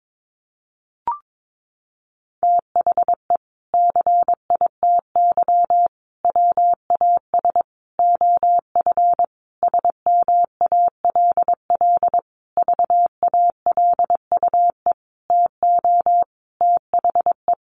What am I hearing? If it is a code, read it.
Morse code, heard as a keyed tone, about 22 wpm, 700 Hz. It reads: THE CITY WAS OF SMALL VALUE TO THE
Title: Sentences from Top 1500 Words 22wpm